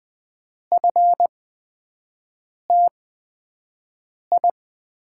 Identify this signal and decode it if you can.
Morse code, heard as a keyed tone, about 20 wpm, spaced as for 5 wpm, 700 Hz. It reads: FTI
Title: USA Call Signs - 5 Characters - 8x Character Spacing - 20wpm